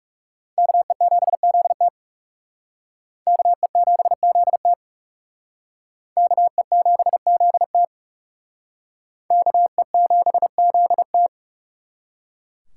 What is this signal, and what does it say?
Morse code, heard as a keyed tone, about 39 wpm, 700 Hz. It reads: KE7ZT KE7ZT KE7ZT KE7ZT